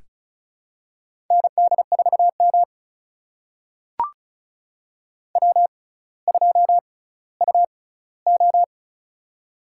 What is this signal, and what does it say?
Morse code, heard as a keyed tone, about 36 wpm, spaced as for 18 wpm, 700 Hz. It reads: ND4M W2UO
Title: USA Call Signs - 6x Character Spacing - 35wpm